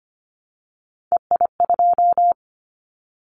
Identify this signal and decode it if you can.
Morse code, heard as a keyed tone, about 25 wpm, 700 Hz. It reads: EI2
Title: Callsign Prefixes for Top 50 DXCC - 25wpm